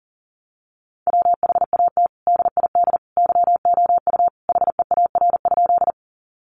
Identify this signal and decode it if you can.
Morse code, heard as a keyed tone, about 40 wpm, 700 Hz. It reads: WHAT DID YOU HEAR?
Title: Sentences from Top 300 Words 40wpm